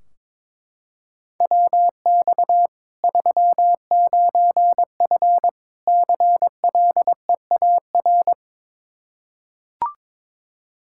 Morse code, heard as a keyed tone, about 22 wpm, 700 Hz. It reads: WX 39F CLEAR
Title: Practice QSO - Exchange 2 - Info - 22wpm